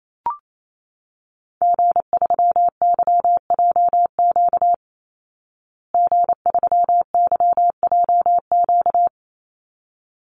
Morse code, heard as a keyed tone, about 28 wpm, 700 Hz. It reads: G3YJQ G3YJQ
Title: Call Signs Repeated 3x - 28wpm